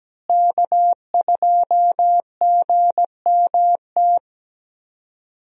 Morse code, heard as a keyed tone, about 17 wpm, 700 Hz. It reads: K2GMT
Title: USA Call Signs - 5 Characters - Repeated 3x - 17wpm